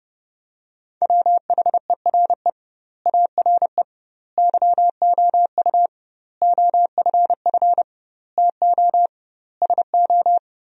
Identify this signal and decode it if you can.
Morse code, heard as a keyed tone, about 30 wpm, 700 Hz. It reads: WHERE ARE YOU OFF TO SO
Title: 2x Word-Spacing - Sentences from Top 400 Words - 30wpm